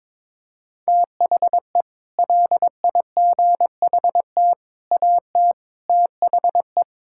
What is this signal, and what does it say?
Morse code, heard as a keyed tone, about 22 wpm, 700 Hz. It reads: THE LIGHT AT THE
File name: practice / 22wpm / Sets of 8 Words - 22wpm